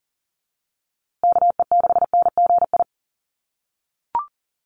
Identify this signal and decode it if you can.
Morse code, heard as a keyed tone, about 40 wpm, 700 Hz. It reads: KE6NGI
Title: USA Call Signs Repeated 2x 40wpm